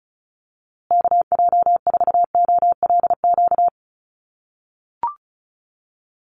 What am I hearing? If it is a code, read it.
Morse code, heard as a keyed tone, about 34 wpm, 700 Hz. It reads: KJ4OLQ
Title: USA Call Signs Repeated 2x 35wpm